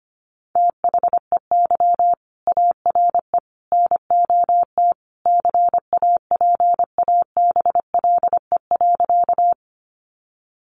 Morse code, heard as a keyed tone, about 25 wpm, 700 Hz. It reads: THEY ARE NOT CAPABLE.
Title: Sentences from Top 1200 Words 25wpm